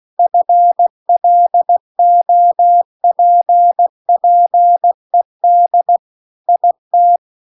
Morse code, heard as a keyed tone, about 16 wpm, 700 Hz. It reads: FLOPPED IT
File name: CW-The-Big-Twoheaded-River-2-16WPM